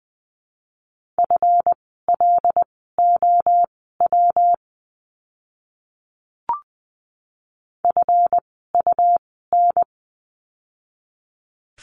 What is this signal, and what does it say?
Morse code, heard as a keyed tone, about 20 wpm, spaced as for 15 wpm, 700 Hz. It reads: FLOW FUN
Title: 139 Lesson 23f - Letter f - Words 20wpm